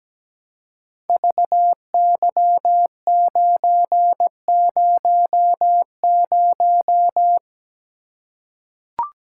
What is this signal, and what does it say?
Morse code, heard as a keyed tone, about 17 wpm, 700 Hz. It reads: VY900
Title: Code Groups - 5 Letter:Numbers - 17wpm